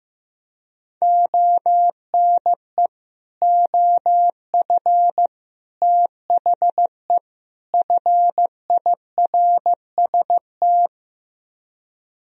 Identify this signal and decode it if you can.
Morse code, heard as a keyed tone, about 15 wpm, 700 Hz. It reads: ONE OF THE FIRST
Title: Speed-Racing - Sets of 4 Words 15wpm